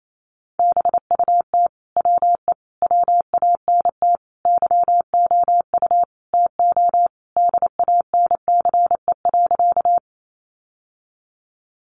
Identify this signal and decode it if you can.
Morse code, heard as a keyed tone, about 28 wpm, 700 Hz. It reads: BUT WE WANT YOU TO DANCE.
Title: Sentences from Top 700 Words 28wpm